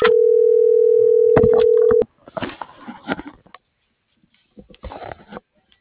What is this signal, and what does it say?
Ringback tone (North America), 440 + 480 Hz.